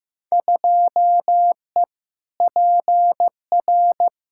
Morse code, heard as a keyed tone, about 15 wpm, 700 Hz. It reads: 2E PR
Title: Field Day Exchanges Encore - 15wpm